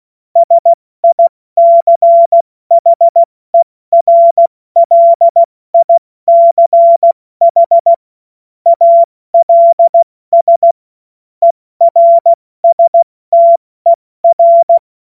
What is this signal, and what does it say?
Morse code, heard as a keyed tone, about 16 wpm, spaced as for 14 wpm, 680 Hz. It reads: SICHERLICH ALS ERSTER
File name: Morsekurs / Witze / Witz 21.November 2023 70 F80